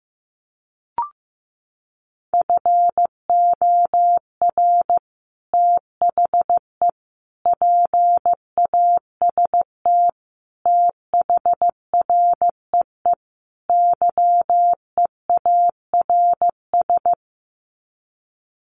Morse code, heard as a keyed tone, about 15 wpm, 700 Hz. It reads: FOR THE PAST THREE YEARS
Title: Sets of 5 Words Encore2 15wpm